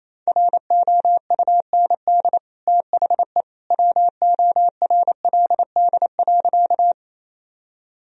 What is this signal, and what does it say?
Morse code, heard as a keyed tone, about 28 wpm, 700 Hz. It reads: ROUND THE WORLD.
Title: Sentences from Top 1000 Words - Review Part 1 - 28wpm